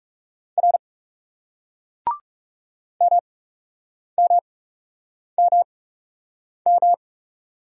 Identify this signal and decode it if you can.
Morse code, heard as a keyed tone, about 40 wpm, spaced as for 8 wpm, 700 Hz. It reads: R MMMM